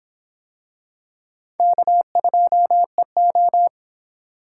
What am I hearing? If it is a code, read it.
Morse code, heard as a keyed tone, about 24 wpm, 700 Hz. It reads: K2EO